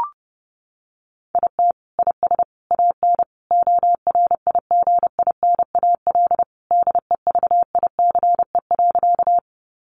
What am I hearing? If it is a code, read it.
Morse code, heard as a keyed tone, about 30 wpm, 700 Hz. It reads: IT IS AN ORIGINAL DEVICE.
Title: Sentences from Top 1500 Words 30wpm